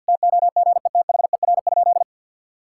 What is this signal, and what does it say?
Morse code, heard as a keyed tone, about 50 wpm, 700 Hz. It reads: TOGETHER?
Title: Sentences from Top 1600 Words 50wpm